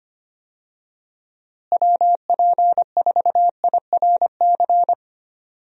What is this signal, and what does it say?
Morse code, heard as a keyed tone, about 25 wpm, 700 Hz. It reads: WP4IRC
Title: USA Call Signs - 6 Characters - Repeated 3x - 25wpm